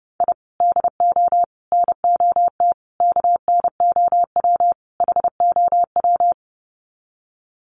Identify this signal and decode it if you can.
Morse code, heard as a keyed tone, about 30 wpm, 700 Hz. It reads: I DO NOT KNOW HOW
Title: Sets of 5 Words 30wpm